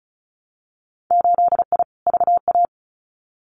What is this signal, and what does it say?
Morse code, heard as a keyed tone, about 34 wpm, 700 Hz. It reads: 8I VA
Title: Winter Field Day Exchanges Encore 35wpm